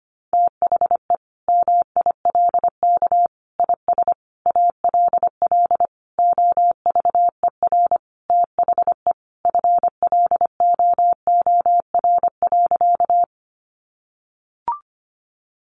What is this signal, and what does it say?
Morse code, heard as a keyed tone, about 25 wpm, 700 Hz. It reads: THE MILK IS ALL OVER THE FLOOR.
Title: Additional Easy Phrases 25wpm